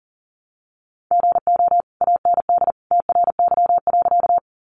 Morse code, heard as a keyed tone, about 40 wpm, 700 Hz. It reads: GO AND TRY.
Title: Sentences from Top 200 Words 40wpm